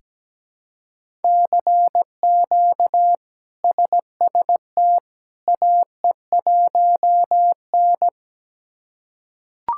Morse code, heard as a keyed tone, about 17 wpm, 700 Hz. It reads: CQ SST AE1N
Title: K1USN SST QSO Practice - 17wpm